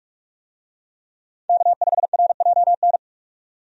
Morse code, heard as a keyed tone, about 46 wpm, 700 Hz. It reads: K5RJN
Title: USA Call Signs - 5 Characters - 45wpm